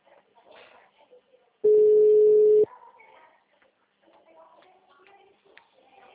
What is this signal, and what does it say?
Ringback tone (Europe, 425 Hz).